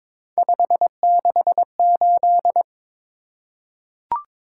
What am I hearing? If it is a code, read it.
Morse code, heard as a keyed tone, about 22 wpm, 700 Hz. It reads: 568
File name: practice / 22wpm / Three Letter-Number Combinations - 22wpm